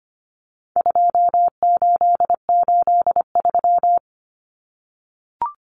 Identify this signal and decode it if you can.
Morse code, heard as a keyed tone, about 25 wpm, 700 Hz. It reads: 2883
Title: Common Cut Numbers - 4-Digit 25wpm